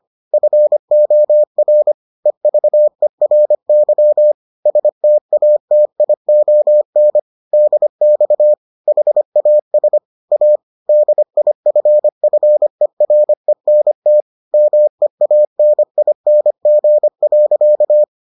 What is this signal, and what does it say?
Morse code, heard as a keyed tone, about 25 wpm, 600 Hz. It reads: FOR EVERY STATION DX HAS A DIFFERENT MEANING.